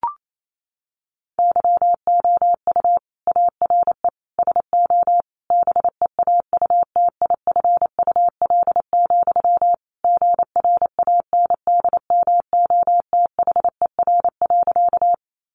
Morse code, heard as a keyed tone, about 28 wpm, 700 Hz. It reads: YOU ARE SO BEAUTIFUL, GRANDMOTHER.
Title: Sentences from Top 1200 Words 28wpm